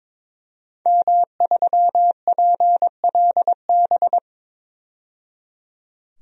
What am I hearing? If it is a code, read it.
Morse code, heard as a keyed tone, about 22 wpm, 700 Hz. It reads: M3PLB
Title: DXCC-100 Call Signs - 5 Characters - 22wpm